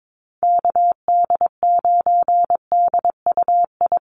Morse code, heard as a keyed tone, about 22 wpm, 700 Hz. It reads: KD9DUI
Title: USA Call Signs Repeated 3x 22wpm